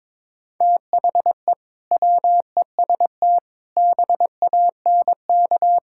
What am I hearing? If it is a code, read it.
Morse code, heard as a keyed tone, about 22 wpm, 700 Hz. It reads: THE WEST BANK
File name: practice / 22wpm / Build A Phrase - 5-Word Phrases - 22wpm